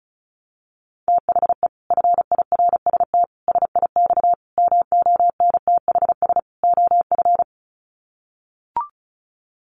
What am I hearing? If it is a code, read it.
Morse code, heard as a keyed tone, about 34 wpm, 700 Hz. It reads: THE FIRST SIX MONTHS OF
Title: Sets of 5 Words Encore2 35wpm